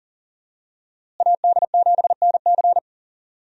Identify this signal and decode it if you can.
Morse code, heard as a keyed tone, about 40 wpm, 700 Hz. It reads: AD7NC
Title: USA Call Signs - 5 Characters - Repeated 3x - 40wpm